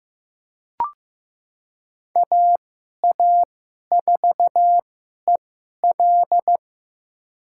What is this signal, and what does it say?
Morse code, heard as a keyed tone, about 15 wpm, spaced as for 11 wpm, 700 Hz. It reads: AA4EL
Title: USA Call Signs - 2x Character Spacing - 15wpm